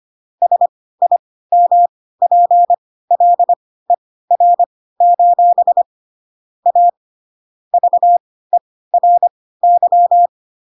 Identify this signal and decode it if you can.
Morse code, heard as a keyed tone, about 25 wpm, spaced as for 16 wpm, 700 Hz. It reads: SIMPLER: A VERY
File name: Pinocchio / Pinocchio ch3 10 25 ik1hge